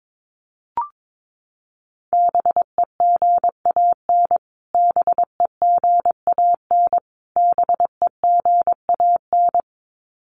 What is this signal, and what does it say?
Morse code, heard as a keyed tone, about 22 wpm, 700 Hz. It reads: BEGAN BEGAN BEGAN
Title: Top 500 Words Repeated 22wpm